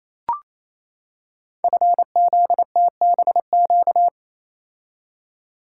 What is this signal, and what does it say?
Morse code, heard as a keyed tone, about 29 wpm, 700 Hz. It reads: FZTBQ
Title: Code Groups - 5 Letters - 28wpm